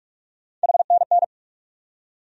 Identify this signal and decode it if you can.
Morse code, heard as a keyed tone, about 45 wpm, 700 Hz. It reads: HNN
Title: USA Call Sign Postfixes - 3 Letters 45wpm